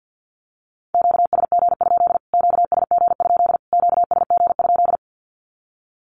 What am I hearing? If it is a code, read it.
Morse code, heard as a keyed tone, about 50 wpm, 700 Hz. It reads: QSG? QSG? QSG?
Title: Q-Codes Repeated 3x 50wpm